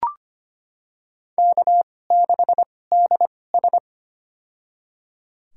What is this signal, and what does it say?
Morse code, heard as a keyed tone, about 25 wpm, spaced as for 18 wpm, 700 Hz. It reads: K6DS
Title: USA Call Signs - 2x Character Spacing - 25wpm